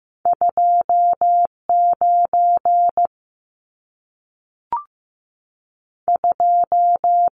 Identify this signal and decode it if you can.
Morse code, heard as a keyed tone, about 15 wpm, 700 Hz. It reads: 29 2
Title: Numbers - 2-Digit 15wpm